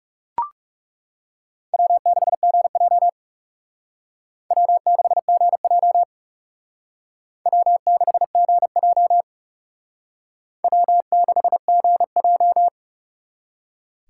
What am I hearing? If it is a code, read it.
Morse code, heard as a keyed tone, about 36 wpm, 700 Hz. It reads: W6GJ W6GJ W6GJ W6GJ
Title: Speed Racing - USA 1x2 Call Signs 30wpm